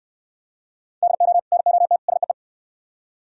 Morse code, heard as a keyed tone, about 68 wpm, 700 Hz. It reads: DO NOT SEE